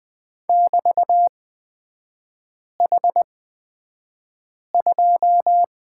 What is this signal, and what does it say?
Morse code, heard as a keyed tone, about 20 wpm, spaced as for 5 wpm, 700 Hz. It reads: <BT> H2
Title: 243 Lesson 39e - Letter z - Three Characters 20wpm